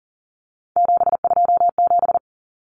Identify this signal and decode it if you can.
Morse code, heard as a keyed tone, about 40 wpm, 700 Hz. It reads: 727
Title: Numbers - 3-Digit 40wpm